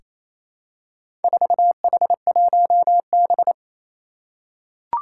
Morse code, heard as a keyed tone, about 28 wpm, 700 Hz. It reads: 4H1B
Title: Four Challenging Letter-Number Combinations 28wpm